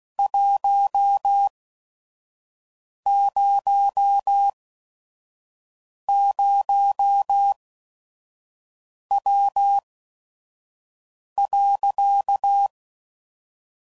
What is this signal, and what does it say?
Morse code, heard as a keyed tone, about 16 wpm, spaced as for 5 wpm, 800 Hz. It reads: 100W.
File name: CW Code Groups @5WPM / QSOs / WD0FBT 2